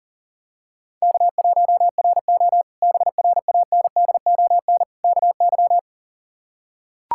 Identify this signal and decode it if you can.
Morse code, heard as a keyed tone, about 40 wpm, 700 Hz. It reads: K1RO BRANDON KY
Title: K1USN SST QSO Practice - 40wpm